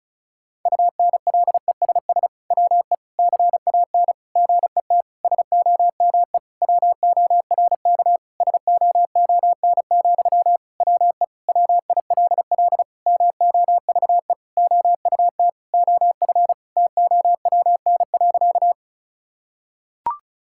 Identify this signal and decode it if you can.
Morse code, heard as a keyed tone, about 34 wpm, 700 Hz. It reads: UNLESS WE CAN GET SOME WORK SOON, WE WILL MOVE OUT OF TOWN.
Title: Sentences from Top 1000 Words - Review Part 6 - 35wpm